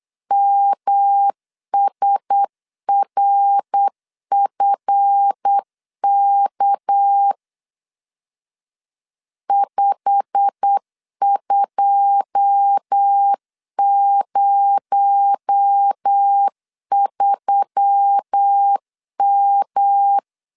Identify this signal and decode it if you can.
Morse code, heard as a keyed tone, about 8 wpm, 800 Hz. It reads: MSRFK 5203M
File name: LEZIONI / 9 2 8wpm 3437